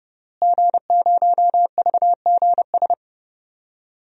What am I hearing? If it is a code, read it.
Morse code, heard as a keyed tone, about 31 wpm, 700 Hz. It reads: G0VGS
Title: DXCC-100 Call Signs - 5 Characters - Repeated 3x - 30wpm